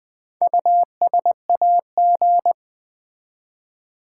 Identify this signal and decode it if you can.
Morse code, heard as a keyed tone, about 20 wpm, 700 Hz. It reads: USAG
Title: Build A Word - 1 Letter At A Time - 3 to 5-Letter Words - 20wpm